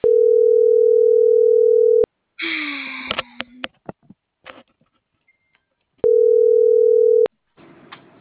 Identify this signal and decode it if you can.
Ringback tone (North America), 440 + 480 Hz.